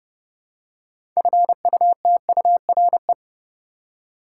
Morse code, heard as a keyed tone, about 30 wpm, 700 Hz. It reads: FUTURE
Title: Six Letter Words - 30wpm